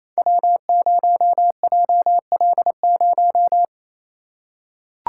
Morse code, heard as a keyed tone, about 28 wpm, 700 Hz. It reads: W0JL0
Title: Code Groups - 5 Letter:Numbers - 28wpm